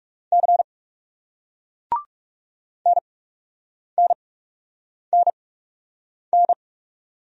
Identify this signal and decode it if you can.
Morse code, heard as a keyed tone, about 39 wpm, spaced as for 8 wpm, 700 Hz. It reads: C NNNN